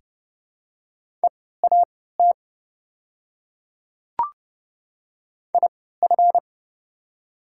Morse code, heard as a keyed tone, about 30 wpm, spaced as for 17 wpm, 700 Hz. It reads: EAT IF
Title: 3x Character Spacing - Top 300 Words - 30wpm